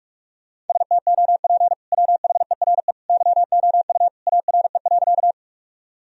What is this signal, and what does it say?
Morse code, heard as a keyed tone, about 42 wpm, 700 Hz. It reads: STOP WHERE YOU ARE.
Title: Sentences from Top 1000 Words - Review Part 1 - 45wpm